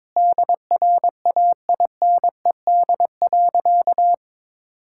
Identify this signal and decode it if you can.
Morse code, heard as a keyed tone, about 22 wpm, 700 Hz. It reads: DRAINED.
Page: morse-code-ninja-practice-22wpm